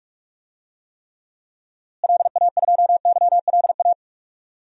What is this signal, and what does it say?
Morse code, heard as a keyed tone, about 42 wpm, 700 Hz. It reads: LA2YLA
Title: DXCC-100 Call Signs - 6 Characters - Repeated 3x - 45wpm